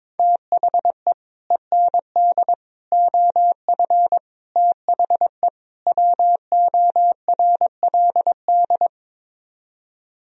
Morse code, heard as a keyed tone, about 22 wpm, 700 Hz. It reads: THE END OF THE WORLD
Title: Speed-Racing - Sets of 5 Words 22wpm